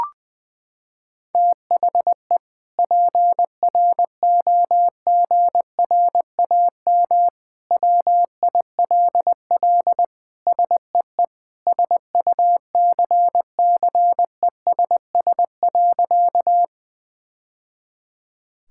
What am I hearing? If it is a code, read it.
Morse code, heard as a keyed tone, about 20 wpm, 700 Hz. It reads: THE PROGRAM WILL SEE SUCCESS.
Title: Sentences from Top 1000 Words - Review Part 5 - 20wpm